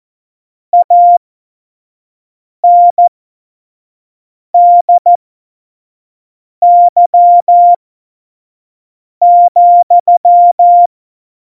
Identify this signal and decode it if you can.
Morse code, heard as a keyed tone, about 14 wpm, spaced as for 5 wpm, 700 Hz. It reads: ANDY,
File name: Morsekurs / Witze / Witz 30. April 2019, 25 F70